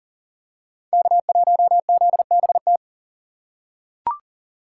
Morse code, heard as a keyed tone, about 40 wpm, 700 Hz. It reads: K1ZBT
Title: USA Call Signs - 5 Characters - 3x Character Spacing - 40wpm